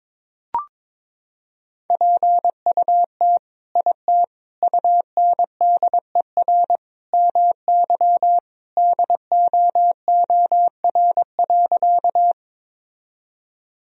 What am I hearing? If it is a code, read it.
Morse code, heard as a keyed tone, about 22 wpm, 700 Hz. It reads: PUT IT UNDER MY DOOR.
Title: Sentences from Top 1000 Words - Review Part 1 - 22wpm